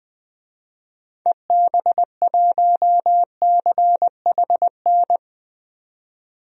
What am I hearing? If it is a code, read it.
Morse code, heard as a keyed tone, about 20 wpm, 700 Hz. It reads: EB1CHN
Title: DXCC-100 Call Signs - 6 Characters - 20wpm